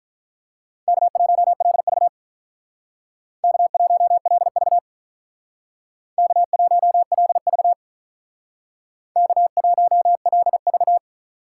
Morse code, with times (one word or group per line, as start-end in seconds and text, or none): 0.9-2.1 K1LV
3.4-4.8 K1LV
6.2-7.7 K1LV
9.2-11.0 K1LV